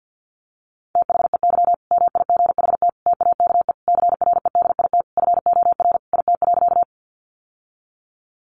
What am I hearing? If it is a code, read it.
Morse code, heard as a keyed tone, about 50 wpm, 700 Hz. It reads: THEY MIGHT TAKE CREDIT FOR IT.
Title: Sentences from Top 1000 Words 50wpm